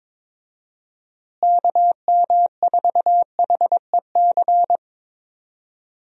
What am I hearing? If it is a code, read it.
Morse code, heard as a keyed tone, about 22 wpm, 700 Hz. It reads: KM4HEC
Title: USA Call Signs - 6 Characters - Repeated 2x - 22wpm